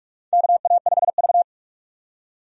Morse code, heard as a keyed tone, about 44 wpm, 700 Hz. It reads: KA5V
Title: USA Call Signs - 4 Characters - 45wpm